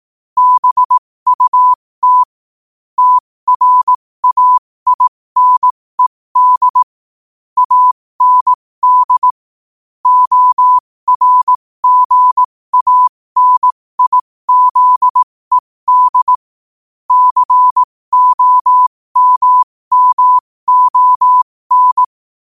Morse code, heard as a keyed tone, about 18 wpm, spaced as for 15 wpm, 1000 Hz. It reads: BUT TRAINED AND ORGANIZED COMMON
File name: qotdfiles / 15wpm